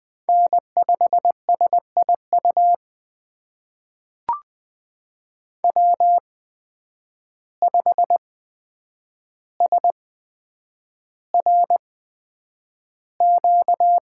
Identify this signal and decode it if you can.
Morse code, heard as a keyed tone, about 20 wpm, spaced as for 9 wpm, 700 Hz. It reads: N5SIU W5SRQ
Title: USA Call Signs - 8x Character Spacing - 20wpm